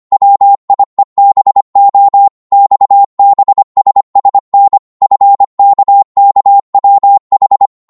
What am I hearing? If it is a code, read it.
Morse code, heard as a keyed tone, about 25 wpm, 800 Hz. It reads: WIEBO XBSSN FKKWH